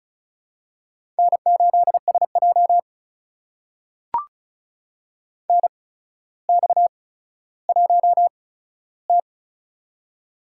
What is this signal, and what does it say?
Morse code, heard as a keyed tone, about 36 wpm, spaced as for 15 wpm, 700 Hz. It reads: N8SJ NX1T
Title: USA Call Signs - 4 Characters - 8x Character Spacing - 35wpm